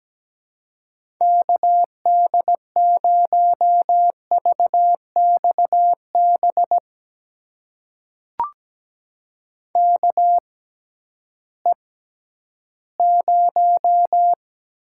Morse code, heard as a keyed tone, about 17 wpm, spaced as for 11 wpm, 700 Hz. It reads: KD0VXB KE0
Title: USA Call Signs - 6 Characters - 6x Character Spacing - 17wpm